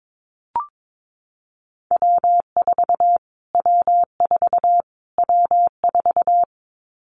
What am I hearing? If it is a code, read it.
Morse code, heard as a keyed tone, about 22 wpm, 700 Hz. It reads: W4 W4 W4
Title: Callsign Prefixes for Top 25 DXCC - Repeated 3x - 22wpm